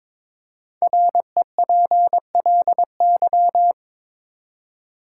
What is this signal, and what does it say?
Morse code, heard as a keyed tone, about 22 wpm, 700 Hz. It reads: REPLY
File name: 5 Letter Words - First 3 Letters Given - 22wpm